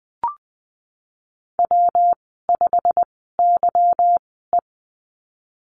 Morse code, heard as a keyed tone, about 20 wpm, spaced as for 15 wpm, 700 Hz. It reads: W5YE
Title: 147 Lesson 24g - Letter y - Call Signs 20wpm